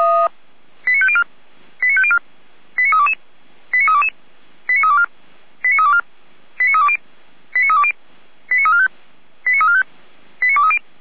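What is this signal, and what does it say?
Five-tone selective call (ZVEI), heard as quick runs of five tones, tones of about 70 ms. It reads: 89504 89504 89322 89322 89324 89324 89320 89320 89345 89345 89230
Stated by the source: ZVEI five-tone selective call received on SDR.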